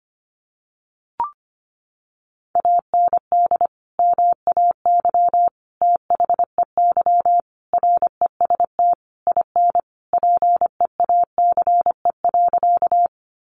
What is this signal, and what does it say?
Morse code, heard as a keyed tone, about 25 wpm, 700 Hz. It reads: AND MAY THEY REST IN PEACE.
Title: Sentences from Top 1500 Words 25wpm